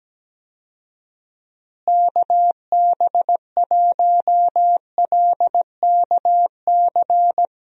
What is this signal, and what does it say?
Morse code, heard as a keyed tone, about 17 wpm, 700 Hz. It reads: KB1LKC
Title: USA Call Signs - 6 Characters - Repeated 2x - 17wpm